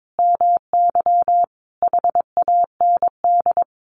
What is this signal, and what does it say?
Morse code, heard as a keyed tone, about 22 wpm, 700 Hz. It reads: MY HAND
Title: Sentences from Top 1500 Words 22wpm